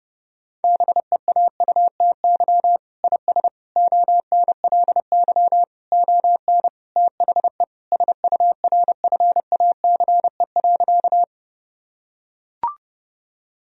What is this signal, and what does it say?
Morse code, heard as a keyed tone, about 29 wpm, 700 Hz. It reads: BEAUTY IS ONLY ON THE SURFACE.
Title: Sentences from Top 1000 Words - Review Part 3 - 30wpm